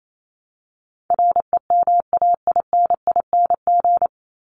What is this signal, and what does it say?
Morse code, heard as a keyed tone, about 28 wpm, 700 Hz. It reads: REMAINING
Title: ING Suffix - Top 100 Words with Base Words 28wpm